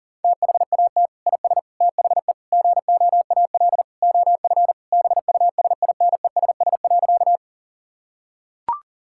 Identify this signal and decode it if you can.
Morse code, heard as a keyed tone, about 40 wpm, 700 Hz. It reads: THAT IS THE GOAL OF BUSINESS.
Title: Sentences from Top 1000 Words - Review Part 5 - 40wpm